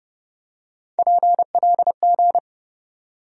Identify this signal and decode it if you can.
Morse code, heard as a keyed tone, about 30 wpm, 700 Hz. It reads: PLG
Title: USA Call Sign Postfixes - 3 Letters 30wpm